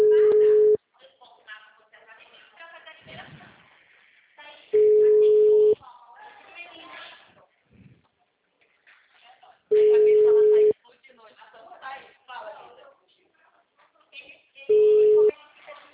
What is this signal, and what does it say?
Ringback tone (Europe, 425 Hz), with speech.
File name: out- -2012- - - .166